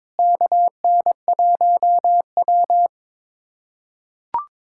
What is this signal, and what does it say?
Morse code, heard as a keyed tone, about 22 wpm, 700 Hz. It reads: KN1W
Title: USA Callsign DE Callsign - 4 Characters - 22wpm